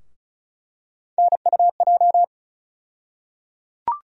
Morse code, heard as a keyed tone, about 36 wpm, 700 Hz. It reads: NUJ
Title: USA Call Sign Postfixes - 3 Letters 35wpm